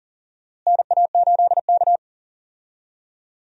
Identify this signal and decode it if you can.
Morse code, heard as a keyed tone, about 40 wpm, 700 Hz. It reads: NA8K
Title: USA Call Signs - 4 Characters - Repeated 3x - 40wpm